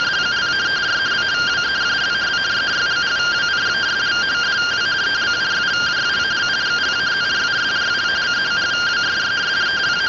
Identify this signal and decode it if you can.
Two-tone teleprinter signal, about 170 Hz shift.